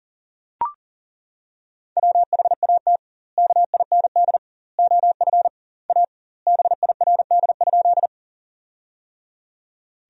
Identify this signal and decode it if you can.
Morse code, heard as a keyed tone, about 40 wpm, 700 Hz. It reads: WHAT KIND OF A BIRD?
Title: 2x Word-Spacing - Sentences from Top 400 Words - 40wpm